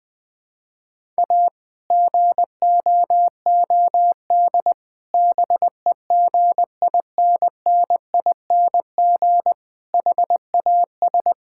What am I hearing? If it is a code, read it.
Morse code, heard as a keyed tone, about 20 wpm, 700 Hz. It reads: A GOOD BEGINNING HAS
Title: Sentences from Top 1000 Words - Review Part 5 - 20wpm